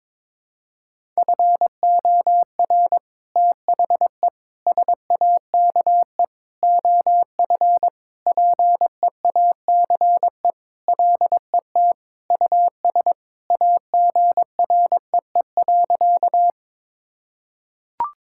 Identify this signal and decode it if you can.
Morse code, heard as a keyed tone, about 22 wpm, 700 Hz. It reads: FOR THE SAKE OF PEACE LET US AGREE.
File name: practice / 22wpm / Sentences with 4-Word Phrases - 22wpm